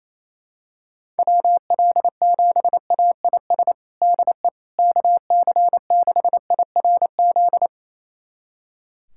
Morse code, heard as a keyed tone, about 28 wpm, 700 Hz. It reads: WL7AIS DE KC6IRZ
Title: USA Callsign DE Callsign 28wpm